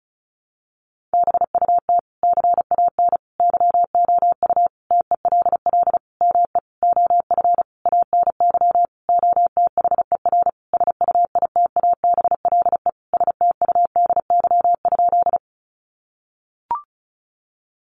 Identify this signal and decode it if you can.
Morse code, heard as a keyed tone, about 34 wpm, 700 Hz. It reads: BUT CAN YOU TELL ME OF ANY OTHER SUITABLE STUDY?
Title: Sentences from Top 1400 Words 35wpm